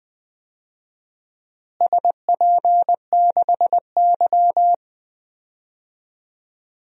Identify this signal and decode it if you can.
Morse code, heard as a keyed tone, about 20 wpm, 700 Hz. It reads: SP6Y
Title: DXCC-100 Call Signs - 4 Characters - Repeated 2x - 20wpm